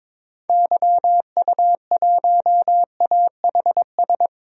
Morse code, heard as a keyed tone, about 22 wpm, 700 Hz. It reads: YU1AHS
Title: DXCC-100 Call Signs - 6 Characters - Repeated 3x - 22wpm